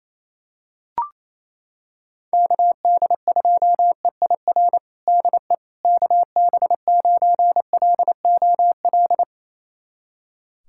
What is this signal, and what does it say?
Morse code, heard as a keyed tone, about 28 wpm, 700 Hz. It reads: KD2EIR DE KB9LOL
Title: USA Callsign DE Callsign - 6 Characters - 28wpm